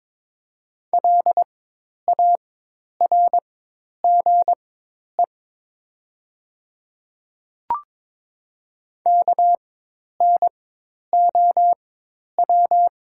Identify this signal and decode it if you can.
Morse code, heard as a keyed tone, about 22 wpm, spaced as for 10 wpm, 700 Hz. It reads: LARGE KNOW
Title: 4x Character Spacing - Top 300 Words - 22wpm